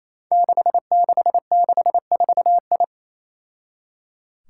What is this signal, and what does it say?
Morse code, heard as a keyed tone, about 28 wpm, 700 Hz. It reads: 6664I
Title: Five Challenging Letter-Number Combinations 28wpm